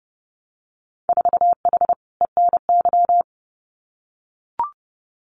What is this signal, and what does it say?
Morse code, heard as a keyed tone, about 30 wpm, 700 Hz. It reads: 4H ENY
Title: Winter Field Day Exchanges Encore 30wpm